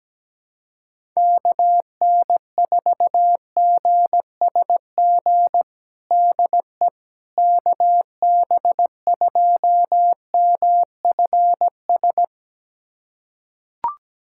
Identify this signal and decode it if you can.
Morse code, heard as a keyed tone, about 17 wpm, 700 Hz. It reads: KN4GSG DE KB2MFS
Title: USA Callsign DE Callsign - 6 Characters - 17wpm